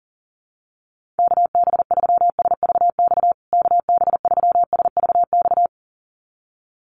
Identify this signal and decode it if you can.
Morse code, heard as a keyed tone, about 40 wpm, 700 Hz. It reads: KB3SVX KB3SVX
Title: USA Call Signs Repeated 2x 40wpm